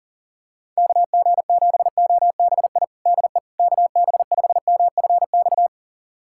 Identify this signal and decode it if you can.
Morse code, heard as a keyed tone, about 40 wpm, 700 Hz. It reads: KG7OBI DE KB5MFX
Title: USA Callsign DE Callsign - 6 Characters - 40wpm